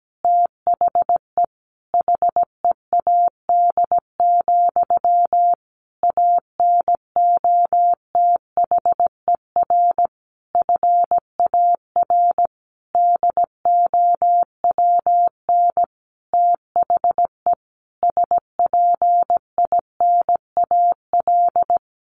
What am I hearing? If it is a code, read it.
Morse code, heard as a keyed tone, about 17 wpm, 700 Hz. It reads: THE HEAD, ANOTHER FAR DOWN THE SPINAL
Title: A Sound Of Thunder 17wpm